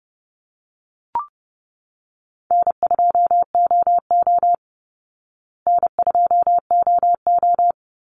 Morse code, heard as a keyed tone, about 29 wpm, 700 Hz. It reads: N2OO N2OO
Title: Easy Call Signs Repeated 2x - 30wpm